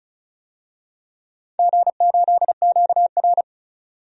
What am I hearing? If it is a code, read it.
Morse code, heard as a keyed tone, about 36 wpm, 700 Hz. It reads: G8QR
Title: DXCC-100 Call Signs - 4 Characters - 35wpm